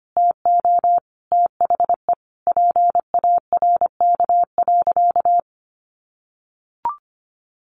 Morse code, heard as a keyed tone, about 25 wpm, 700 Hz. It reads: TO THE PARK.
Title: Sentences from Top 1400 Words 25wpm